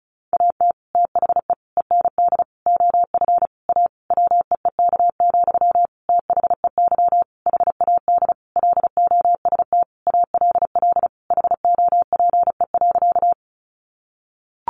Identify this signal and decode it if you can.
Morse code, heard as a keyed tone, about 34 wpm, 700 Hz. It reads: AT THE END OF A WEEK, THEY HAD LOST ALL HOPE.
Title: Sentences from Top 1400 Words 35wpm